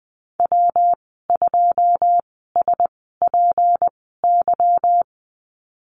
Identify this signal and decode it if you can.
Morse code, heard as a keyed tone, about 20 wpm, spaced as for 15 wpm, 700 Hz. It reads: W2SPY
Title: 193 Lesson 31g - Letter v - Call Signs 20wpm